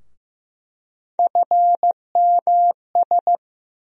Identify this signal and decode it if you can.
Morse code, heard as a keyed tone, about 15 wpm, 700 Hz. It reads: FMS
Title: USA Call Sign Postfixes - 3 Letters 15wpm